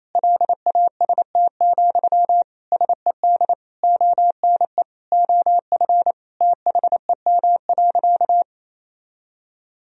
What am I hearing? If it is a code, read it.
Morse code, heard as a keyed tone, about 28 wpm, 700 Hz. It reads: LAST, SED ONE OF THEM.
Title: Sentences from Top 300 Words and Abbreviations 28wpm